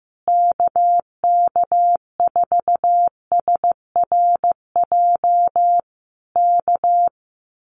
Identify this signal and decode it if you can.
Morse code, heard as a keyed tone, about 15 wpm, 700 Hz. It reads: KK4SRJ K
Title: Common QSO Phrases Encore 15wpm